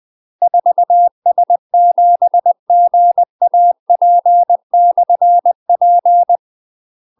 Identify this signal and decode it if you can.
Morse code, heard as a keyed tone, about 20 wpm, 700 Hz. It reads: 4S7GAP/P
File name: UK MASTER QSOs TRAINING / 20WPM / UK MASTER QSOs 0073